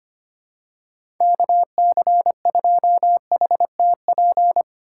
Morse code, heard as a keyed tone, about 25 wpm, 700 Hz. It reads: KC2HTP
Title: USA Call Signs - 6 Characters - 3x Character Spacing - 25wpm